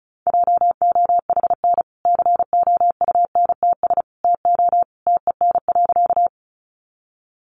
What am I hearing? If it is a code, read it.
Morse code, heard as a keyed tone, about 34 wpm, 700 Hz. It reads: JOHN COUNTS TO TEN.
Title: Easy Phrases 35wpm